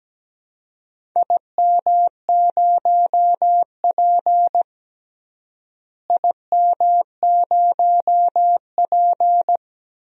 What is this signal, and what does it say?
Morse code, heard as a keyed tone, about 17 wpm, 700 Hz. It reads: IM0P IM0P
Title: DXCC-100 Call Signs - 4 Characters - Repeated 3x - 17wpm